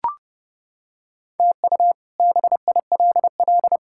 Morse code, heard as a keyed tone, about 31 wpm, 700 Hz. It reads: TU BILL
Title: K1USN SST QSO Practice - 30wpm